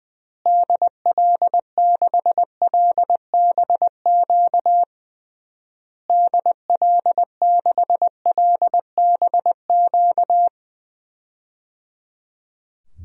Morse code, heard as a keyed tone, about 20 wpm, 700 Hz. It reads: DL6LBQ DL6LBQ
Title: DXCC-100 Call Signs - 6 Characters - Repeated 3x - 20wpm